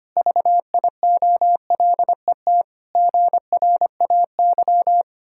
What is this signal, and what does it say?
Morse code, heard as a keyed tone, about 25 wpm, 700 Hz. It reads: VIOLET GRAY
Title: Pangram Challenge v2 - 25wpm